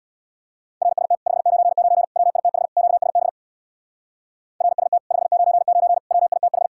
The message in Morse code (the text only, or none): RST 599 WITH QSB RST 599 WITH